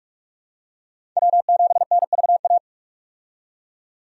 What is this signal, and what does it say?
Morse code, heard as a keyed tone, about 46 wpm, 700 Hz. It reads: W7NVA
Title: USA Call Signs - 5 Characters - 45wpm